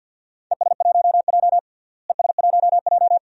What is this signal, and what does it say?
Morse code, heard as a keyed tone, about 50 wpm, 700 Hz. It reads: ES1J ES1J
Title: DXCC-100 Call Signs - 4 Characters - Repeated 2x - 50wpm